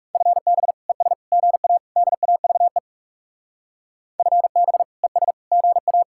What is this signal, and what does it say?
Morse code, heard as a keyed tone, about 37 wpm, 700 Hz. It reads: FB ES GA DAVE FB ES GA
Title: Speed-Racing Common QSO Phrases - 30wpm